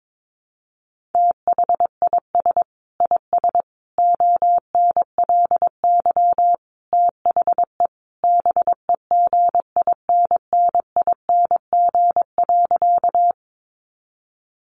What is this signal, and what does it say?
Morse code, heard as a keyed tone, about 22 wpm, 700 Hz. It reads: THIS IS ONLY THE BEGINNING.
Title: Sentences from Top 800 Words 22wpm